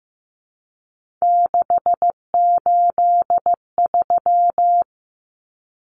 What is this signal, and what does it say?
Morse code, heard as a keyed tone, about 15 wpm, 700 Hz. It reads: 683
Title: Numbers - 3-Digit 15wpm